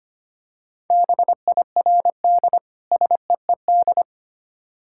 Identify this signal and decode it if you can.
Morse code, heard as a keyed tone, about 25 wpm, 700 Hz. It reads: BIRD SEED
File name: practice / 25wpm / Difficult Words with Hints - 25wpm